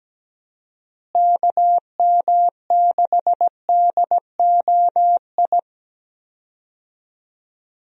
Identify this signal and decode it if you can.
Morse code, heard as a keyed tone, about 17 wpm, 700 Hz. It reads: KM6DOI
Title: USA Call Signs - 6 Characters - Repeated 3x - 17wpm